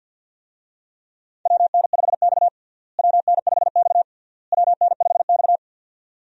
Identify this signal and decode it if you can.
Morse code, heard as a keyed tone, about 52 wpm, 700 Hz. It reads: WN5X WN5X WN5X
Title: USA Call Signs - 4 Characters - Repeated 3x - 50wpm